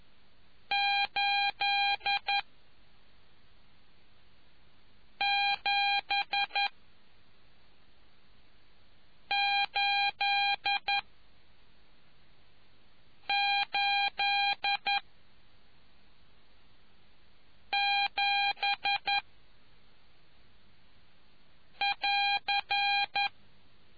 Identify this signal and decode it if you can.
Morse code, heard as a keyed tone, about 11 wpm, 800 Hz. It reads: 8 7 8 8 7 <AR>